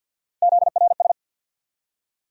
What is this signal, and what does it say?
Morse code, heard as a keyed tone, about 51 wpm, 700 Hz. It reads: ZRS
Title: USA Call Sign Postfixes - 3 Letters 50wpm